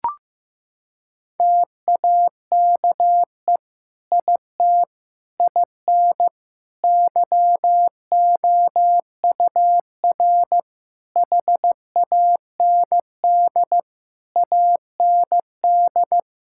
Morse code, heard as a keyed tone, about 15 wpm, 700 Hz. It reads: TAKE IT IN YOUR HAND AND
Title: Sentences from Top 1600 Words 15wpm